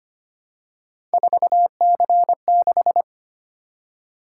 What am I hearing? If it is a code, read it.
Morse code, heard as a keyed tone, about 25 wpm, 700 Hz. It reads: 4C6
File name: Three Letter-Number Combinations - 25wpm